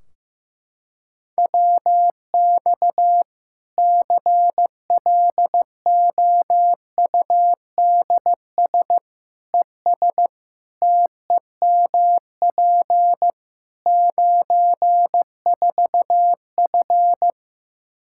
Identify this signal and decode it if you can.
Morse code, heard as a keyed tone, about 16 wpm, 700 Hz. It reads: WX CLOUDS ES TEMP 94F
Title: K1USN SST QSO Practice 15wpm